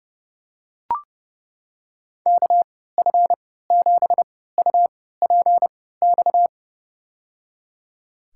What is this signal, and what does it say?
Morse code, heard as a keyed tone, about 32 wpm, spaced as for 17 wpm, 700 Hz. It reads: KF7UPX
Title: USA Call Signs - 6 Characters - 3x Character Spacing - 30wpm